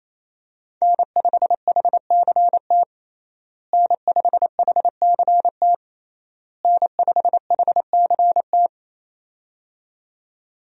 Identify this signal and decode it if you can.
Morse code, heard as a keyed tone, about 27 wpm, 700 Hz. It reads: N5HCT N5HCT N5HCT
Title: USA Call Signs - 5 Characters - Repeated 3x - 28wpm